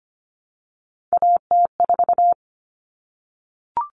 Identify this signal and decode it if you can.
Morse code, heard as a keyed tone, about 25 wpm, 700 Hz. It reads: AT4
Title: Common Cut Numbers - 3-Digit 25wpm